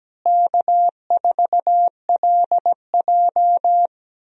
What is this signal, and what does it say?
Morse code, heard as a keyed tone, about 17 wpm, 700 Hz. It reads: K4LJ
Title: USA Call Signs - 4 Characters - Repeated 3x - 17wpm